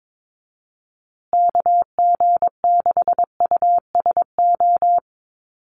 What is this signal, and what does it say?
Morse code, heard as a keyed tone, about 22 wpm, 700 Hz. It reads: KG6USO
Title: USA Call Signs Repeated 3x 22wpm